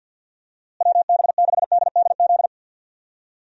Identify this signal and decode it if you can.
Morse code, heard as a keyed tone, about 50 wpm, 700 Hz. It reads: WB6DDZ
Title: USA Call Signs - 6 Characters - Repeated 3x - 50wpm